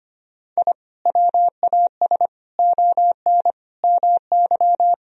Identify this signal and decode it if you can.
Morse code, heard as a keyed tone, about 25 wpm, 700 Hz. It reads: I WAS ON MY
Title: Sentences from Top 1000 Words - Review Part 3 - 25wpm